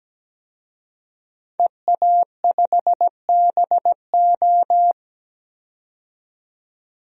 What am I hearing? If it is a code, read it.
Morse code, heard as a keyed tone, about 17 wpm, 700 Hz. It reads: EA5BO
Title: DXCC-100 Call Signs - 5 Characters - 17wpm